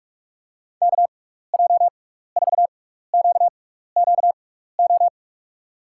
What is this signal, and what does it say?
Morse code, heard as a keyed tone, about 42 wpm, spaced as for 15 wpm, 700 Hz. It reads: KJ4QQO
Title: USA Call Signs - 6 Characters - 6x Character Spacing - 45wpm